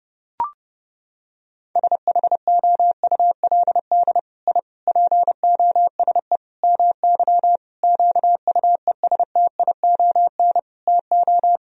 Morse code, heard as a keyed tone, about 30 wpm, 700 Hz. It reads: SHOULD I POSE MY QUESTION TO
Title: Sentences from Top 1000 Words - Review Part 2 - 30wpm